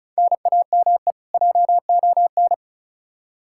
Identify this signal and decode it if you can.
Morse code, heard as a keyed tone, about 34 wpm, 700 Hz. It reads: NAME JON
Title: K1USN SST QSO Practice 35wpm